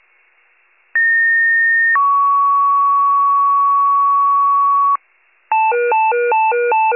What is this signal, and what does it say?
Two-tone page, heard as a one-second tone, then a three-second tone, about 1850 then 1120 Hz. It is heard from 0.9 to 5.0 s.